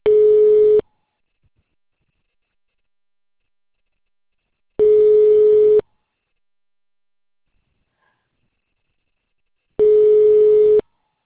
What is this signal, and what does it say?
Ringback tone (Europe, 425 Hz).